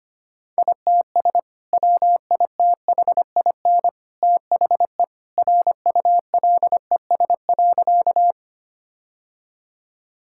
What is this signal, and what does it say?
Morse code, heard as a keyed tone, about 25 wpm, 700 Hz. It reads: ITS WITHIN THE RULES.
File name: practice / 25wpm / Sentences with 2-Word Phrases - 25wpm